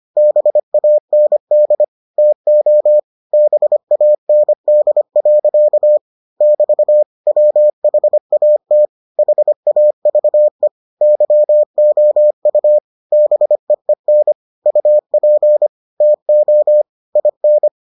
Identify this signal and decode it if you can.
Morse code, heard as a keyed tone, about 25 wpm, 600 Hz. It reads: BAND TO BAND. <BT> WHAT HAVE YOU BEEN UP TO IN